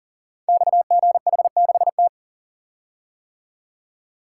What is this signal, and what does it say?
Morse code, heard as a keyed tone, about 43 wpm, 690 Hz. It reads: XGH6T
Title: Code Groups - 5 Letter:Numbers - 40wpm